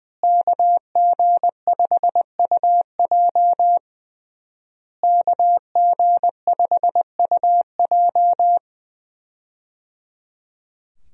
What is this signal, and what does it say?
Morse code, heard as a keyed tone, about 20 wpm, 700 Hz. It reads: KG5UJ KG5UJ
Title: USA Call Signs - 5 Characters - Repeated 3x - 20wpm